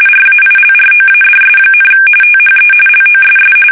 RTTY, about 45.45 baud. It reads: RAILROADS, THE LOSS
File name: fsk850 / usnavy3